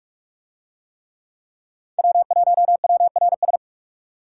Morse code, heard as a keyed tone, about 49 wpm, 700 Hz. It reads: W1WRS